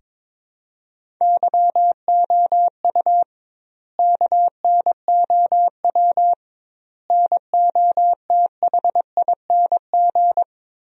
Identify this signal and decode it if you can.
Morse code, heard as a keyed tone, about 22 wpm, 700 Hz. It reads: YOU KNOW NOTHING
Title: 2x Word-Spacing - Sentences from Top 500 Words - 22wpm